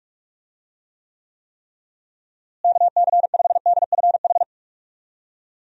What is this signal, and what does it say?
Morse code, heard as a keyed tone, about 41 wpm, 700 Hz. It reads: KC5DFH
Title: USA Call Signs - 6 Characters - Repeated 3x - 45wpm